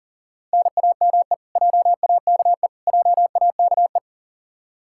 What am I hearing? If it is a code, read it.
Morse code, heard as a keyed tone, about 40 wpm, 700 Hz. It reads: NAME JAKE JAKE
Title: Practice QSO - Exchange 1 - Beginning - 40wpm